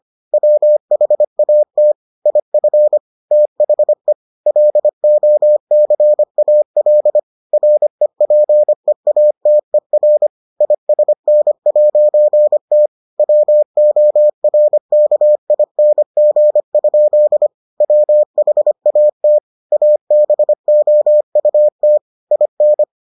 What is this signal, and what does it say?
Morse code, heard as a keyed tone, about 25 wpm, 600 Hz. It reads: WHAT IF THE LOCAL REPEATER ISN'T WORKING? WHAT ABOUT IN